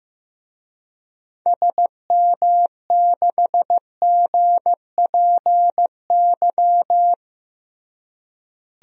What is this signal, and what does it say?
Morse code, heard as a keyed tone, about 16 wpm, 700 Hz. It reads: SM6GPY
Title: DXCC-100 Call Signs - 6 Characters - Repeated 3x - 15wpm